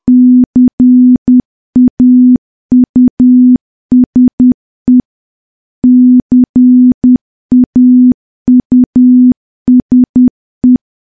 Morse code, heard as a keyed tone, about 10 wpm, 250 Hz. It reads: CAUSE CAUSE
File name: morse / morse-training-10WPM